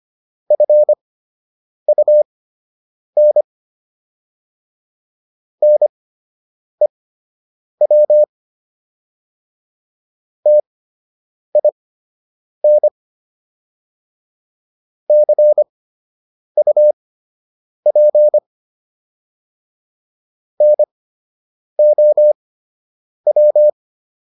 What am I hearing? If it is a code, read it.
Morse code, heard as a keyed tone, about 25 wpm, spaced as for 8 wpm, 600 Hz. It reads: FUN NEW TIN CUP NOW